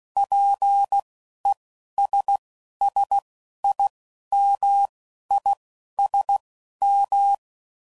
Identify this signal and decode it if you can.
Morse code, heard as a keyed tone, about 16 wpm, spaced as for 12 wpm, 800 Hz. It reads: PESSIMISM